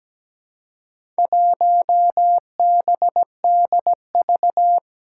Morse code, heard as a keyed tone, about 17 wpm, 700 Hz. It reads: 1BDV
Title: Four Challenging Letter-Number Combinations 17wpm